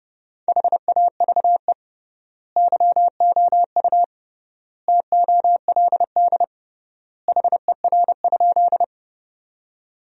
Morse code, heard as a keyed tone, about 30 wpm, 700 Hz. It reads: HAVE YOU TOLD HER?
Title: 3x Word-Spacing - Sentences from Top 400 Words - 30wpm